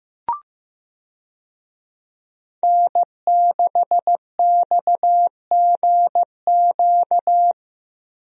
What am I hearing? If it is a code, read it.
Morse code, heard as a keyed tone, about 15 wpm, 700 Hz. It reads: N6XGQ
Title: USA Call Signs - 5 Characters - Repeated 3x - 15wpm